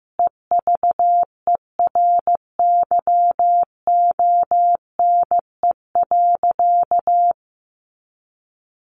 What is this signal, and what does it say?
Morse code, heard as a keyed tone, about 15 wpm, 700 Hz. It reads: EVERYONE.
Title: A Sound Of Thunder 15wpm